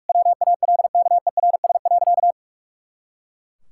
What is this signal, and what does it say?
Morse code, heard as a keyed tone, about 41 wpm, 700 Hz. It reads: WALKERS.
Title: Pangram Challenge v2 - 45wpm